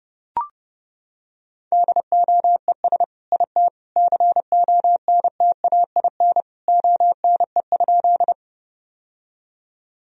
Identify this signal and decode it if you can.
Morse code, heard as a keyed tone, about 30 wpm, 700 Hz. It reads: DOES IT CONTAIN ONE?
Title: Sentences from Top 1000 Words - Review Part 2 - 30wpm